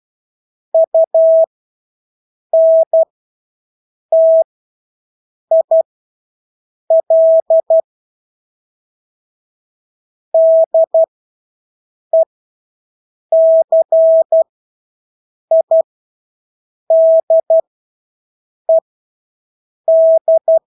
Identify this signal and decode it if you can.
Morse code, heard as a keyed tone, about 12 wpm, spaced as for 6 wpm, 650 Hz. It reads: UNTIL DECIDED
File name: Engelske ord 12-6 wpm